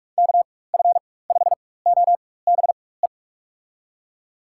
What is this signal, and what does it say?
Morse code, heard as a keyed tone, about 42 wpm, spaced as for 20 wpm, 700 Hz. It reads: KF5OBE
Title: USA Call Signs - 6 Characters - 4x Character Spacing - 45wpm